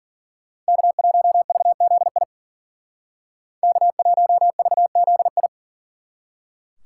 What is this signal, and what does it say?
Morse code, heard as a keyed tone, about 51 wpm, 700 Hz. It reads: K1VZI K1VZI